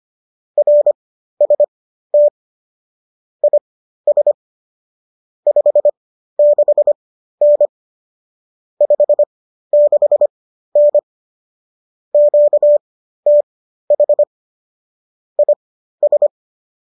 Morse code, heard as a keyed tone, about 25 wpm, spaced as for 13 wpm, 590 Hz. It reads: RST IS 56N 56N QTH IS
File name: qso213 13